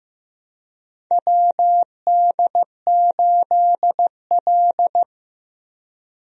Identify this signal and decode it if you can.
Morse code, heard as a keyed tone, about 15 wpm, 700 Hz. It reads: WD8L
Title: USA Call Signs - 4 Characters - Repeated 2x - 15wpm